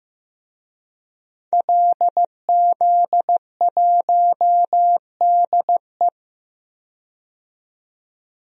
Morse code, heard as a keyed tone, about 15 wpm, 700 Hz. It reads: LZ1DE
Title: DXCC-100 Call Signs - 5 Characters - 15wpm